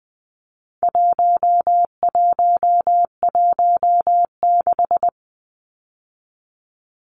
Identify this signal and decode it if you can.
Morse code, heard as a keyed tone, about 20 wpm, 700 Hz. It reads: 1116
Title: Numbers - 4-Digit 20wpm